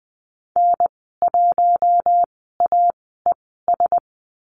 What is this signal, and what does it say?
Morse code, heard as a keyed tone, about 20 wpm, 700 Hz. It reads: N 1 A E S
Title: 062 Lesson 11g - Letter h - Call Signs 20wpm